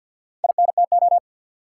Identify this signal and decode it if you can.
Morse code, heard as a keyed tone, about 53 wpm, 700 Hz. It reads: INTO
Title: Sets of 2 Words - New Word Drill - 50wpm